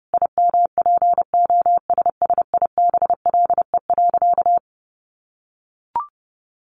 Morse code, heard as a keyed tone, about 30 wpm, 700 Hz. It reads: IMPOSSIBLE.
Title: Sentences from Top 1500 Words 30wpm